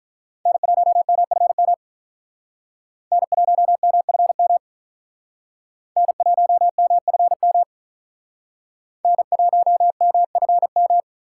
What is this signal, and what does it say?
Morse code, heard as a keyed tone, about 45 wpm, 700 Hz. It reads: N1MFM N1MFM N1MFM N1MFM